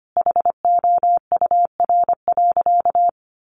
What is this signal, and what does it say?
Morse code, heard as a keyed tone, about 25 wpm, 700 Hz. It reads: HOUR.
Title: Sentences from Top 1300 Words 25wpm